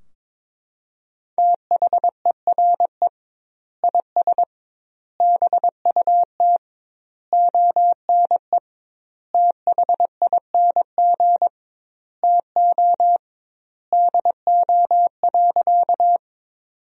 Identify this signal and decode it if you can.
Morse code, heard as a keyed tone, about 22 wpm, 700 Hz. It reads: THERE IS BUT ONE THING TO DO.
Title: 2x Word-Spacing - Sentences from Top 100 Words - 22wpm